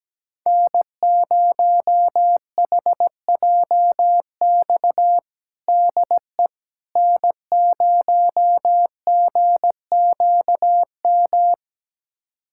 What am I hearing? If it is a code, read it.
Morse code, heard as a keyed tone, about 17 wpm, 700 Hz. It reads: N0HJX DE N0GQM
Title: USA Callsign DE Callsign - 5 Characters - 17wpm